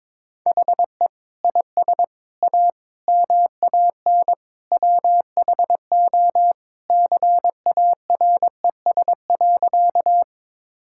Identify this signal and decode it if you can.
Morse code, heard as a keyed tone, about 22 wpm, 700 Hz. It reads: HE IS A MAN WHO CARES.
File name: practice / 22wpm / Sentences with 3-Word Phrases - 22wpm